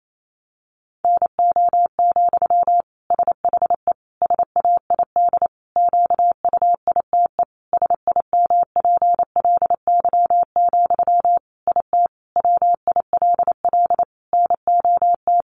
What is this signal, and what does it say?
Morse code, heard as a keyed tone, about 28 wpm, 700 Hz. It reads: NO, SHE SAID QUITE SIMPLY, IT WILL NOT
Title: Sentences from Top 800 Words 28wpm